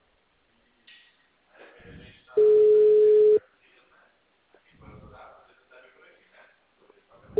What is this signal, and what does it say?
Ringback tone (Europe, 425 Hz), with speech.